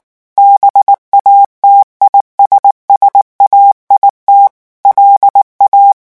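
Morse code, heard as a keyed tone, about 19 wpm, 800 Hz. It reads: BATISSAIT LA
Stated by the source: French Morse revision 8 drill of random five-letter groups at 18 words per minute.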